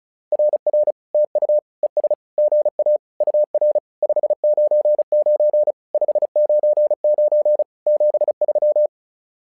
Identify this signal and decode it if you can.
Morse code, heard as a keyed tone, about 34 wpm, 600 Hz. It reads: RR TU ES GA UR 599 599 73
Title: SOTA QSO Practice 35wpm